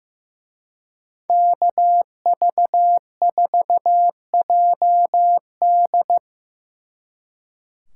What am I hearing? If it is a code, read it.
Morse code, heard as a keyed tone, about 18 wpm, 700 Hz. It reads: KV4JD
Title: Speed Racing - USA 2x2 Call Signs 15wpm